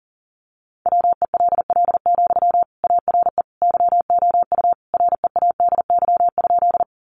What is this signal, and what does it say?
Morse code, heard as a keyed tone, about 40 wpm, 700 Hz. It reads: WELL, ARE YOU READY?
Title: Sentences from Top 400 Words 40wpm